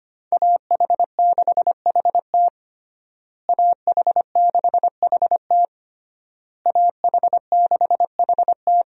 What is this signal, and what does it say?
Morse code, heard as a keyed tone, about 25 wpm, 700 Hz. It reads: AH6HT AH6HT AH6HT
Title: USA Call Signs - 5 Characters - Repeated 3x - 25wpm